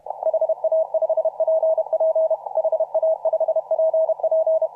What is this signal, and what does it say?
Morse code, heard as a keyed tone, about 32 wpm, 640 Hz. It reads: HA5PP HA5PP
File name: CW / RHOB CW 2017 HA6P 0001